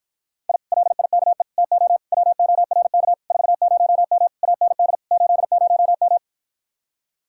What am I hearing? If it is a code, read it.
Morse code, heard as a keyed tone, about 54 wpm, 700 Hz. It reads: I LIKE TO WORK 40M AND 80M